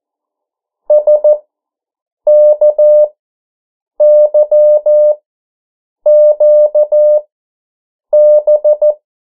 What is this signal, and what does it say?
Morse code, heard as a keyed tone, about 14 wpm, spaced as for 7 wpm, 600 Hz. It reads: SKYQB
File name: morse mp3 / 09 urok 8